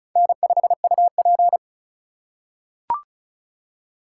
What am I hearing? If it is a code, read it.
Morse code, heard as a keyed tone, about 34 wpm, 700 Hz. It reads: N5UP
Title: USA Call Signs - 4 Characters - 4x Character Spacing - 35wpm